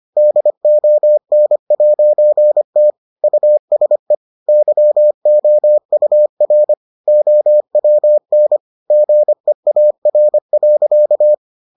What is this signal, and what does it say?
Morse code, heard as a keyed tone, about 25 wpm, 600 Hz. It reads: DON'T USE YOUR OWN GEAR.